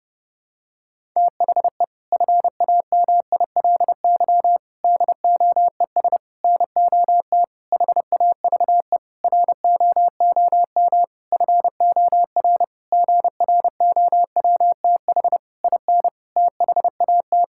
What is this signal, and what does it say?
Morse code, heard as a keyed tone, about 30 wpm, 700 Hz. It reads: THE FAMILY DOES NOT HAVE ROOM FOR GROWTH IN THAT
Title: Sentences from Top 1000 Words - Review Part 5 - 30wpm